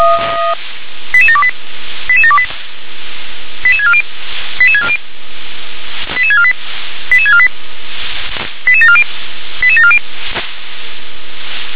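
Five-tone selective call (ZVEI), heard as quick runs of five tones, tones of about 70 ms. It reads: 88728 88728 88640 88640 88648 88648 88740 88740
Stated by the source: Received on SDR (ZVEI).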